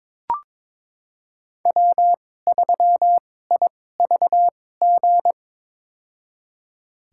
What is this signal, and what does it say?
Morse code, heard as a keyed tone, about 22 wpm, spaced as for 16 wpm, 700 Hz. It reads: W3IVG
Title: USA Call Signs - 2x Character Spacing - 22wpm